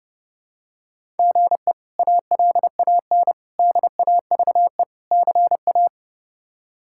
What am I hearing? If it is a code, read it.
Morse code, heard as a keyed tone, about 30 wpm, 700 Hz. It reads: GE ALAN DAVE CA
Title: K1USN SST QSO Practice - 30wpm